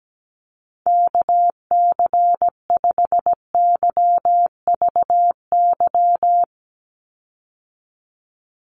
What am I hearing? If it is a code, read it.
Morse code, heard as a keyed tone, about 17 wpm, 700 Hz. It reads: KC5YVY
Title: USA Call Signs 17wpm